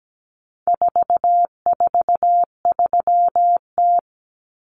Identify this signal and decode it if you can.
Morse code, heard as a keyed tone, about 17 wpm, 700 Hz. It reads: 443T
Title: Common Cut Numbers - 4-Digit 17wpm